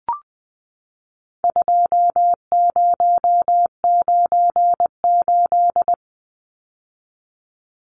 Morse code, heard as a keyed tone, about 20 wpm, 700 Hz. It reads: 2098
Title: Numbers - 4-Digit 20wpm